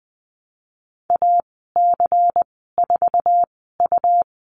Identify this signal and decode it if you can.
Morse code, heard as a keyed tone, about 20 wpm, spaced as for 15 wpm, 700 Hz. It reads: AC4U
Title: 140 Lesson 23g - Letter f - Call Signs 20wpm